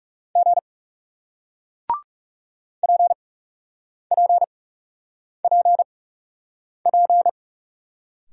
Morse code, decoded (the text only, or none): G PPPP